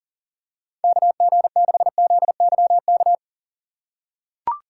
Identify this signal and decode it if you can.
Morse code, heard as a keyed tone, about 40 wpm, 700 Hz. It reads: KG6ZYK
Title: USA Call Signs - 3x Character Spacing - 40wpm